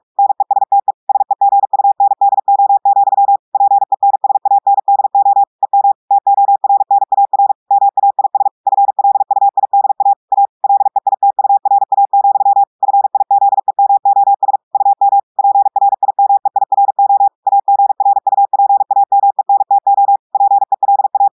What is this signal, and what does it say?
Morse code, heard as a keyed tone, about 46 wpm, 800 Hz. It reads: NESTE SEGUNDO, PENSANDO EM TORNAR MAIS FLUIDA A LEITURA, FIZEMOS UM PRIMEIRO AGRUPAMENTO PELA
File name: portugais texte en clair 1 vitesse 45 mots minutes